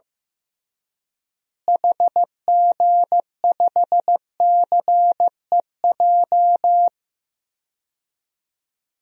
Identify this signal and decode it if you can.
Morse code, heard as a keyed tone, about 15 wpm, 700 Hz. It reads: HG5CEJ
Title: DXCC-100 Call Signs - 6 Characters - Repeated 3x - 15wpm